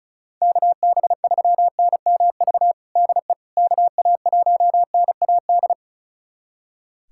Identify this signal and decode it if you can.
Morse code, heard as a keyed tone, about 36 wpm, 700 Hz. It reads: KB3NMV DE KA1NAD
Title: USA Callsign DE Callsign - 6 Characters - 35wpm